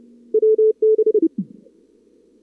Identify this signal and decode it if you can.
Morse code, heard as a keyed tone, about 31 wpm, 440 Hz. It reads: WB
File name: CWSprint / Feb2022 / k5tr-feb-na-sprint-cw-k5ot -0000